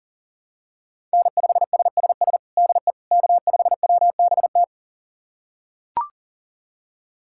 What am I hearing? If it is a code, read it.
Morse code, heard as a keyed tone, about 40 wpm, 700 Hz. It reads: N5SSS DE K5WBT
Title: USA Callsign DE Callsign - 5 Characters - 40wpm